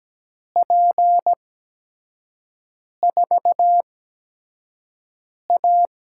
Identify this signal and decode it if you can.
Morse code, heard as a keyed tone, about 17 wpm, spaced as for 5 wpm, 700 Hz. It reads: P4A
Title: USA Call Signs - 5 Characters - 8x Character Spacing - 17wpm